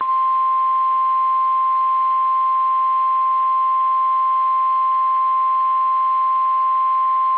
Emergency Alert System (SAME) tones (the 1050 Hz attention tone). It is heard from 0.4 to 6.8 s.
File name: eas / KSDZKDJL / KSDZKDJL